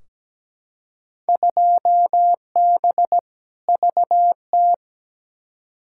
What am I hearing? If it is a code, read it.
Morse code, heard as a keyed tone, about 17 wpm, 700 Hz. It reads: 2B VT
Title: Field Day Exchanges Encore - 17wpm